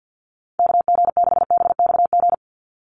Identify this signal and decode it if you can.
Morse code, heard as a keyed tone, about 50 wpm, 700 Hz. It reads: KG6BXG
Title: USA Call Signs 50wpm